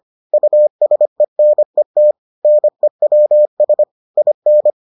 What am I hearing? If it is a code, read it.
Morse code, heard as a keyed tone, about 25 wpm, 600 Hz. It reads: USENET NEWS IN